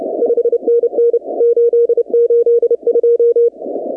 Morse code, heard as a keyed tone, about 30 wpm, 470 Hz. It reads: 5NN 882
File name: contests / 2019-cw-wpx-kt5j-k5pi- -2330 0-15